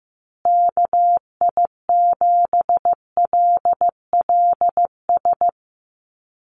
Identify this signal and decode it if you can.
Morse code, heard as a keyed tone, about 15 wpm, 700 Hz. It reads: KI7LLS
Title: USA Call Signs Repeated 3x 15wpm